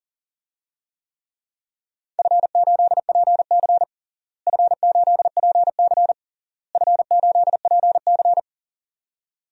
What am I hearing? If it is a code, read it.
Morse code, heard as a keyed tone, about 40 wpm, 700 Hz. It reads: F8PC F8PC F8PC
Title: DXCC-100 Call Signs - 4 Characters - Repeated 3x - 40wpm